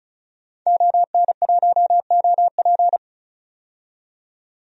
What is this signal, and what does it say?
Morse code, heard as a keyed tone, about 36 wpm, 700 Hz. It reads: ON1OP
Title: DXCC-100 Call Signs - 5 Characters - 35wpm